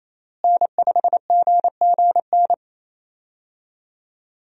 Morse code, heard as a keyed tone, about 28 wpm, 700 Hz. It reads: N5GGN
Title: USA Call Signs - 5 Characters - Repeated 3x - 28wpm